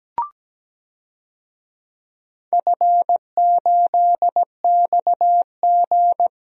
Morse code, heard as a keyed tone, about 17 wpm, 700 Hz. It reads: F8XG
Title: DXCC-100 Call Signs - 4 Characters - 17wpm